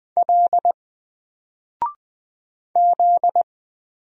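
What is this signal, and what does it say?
Morse code, heard as a keyed tone, about 20 wpm, 700 Hz. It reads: L Z
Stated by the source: Single letter and number Morse drill at 20 wpm, from N4LCD.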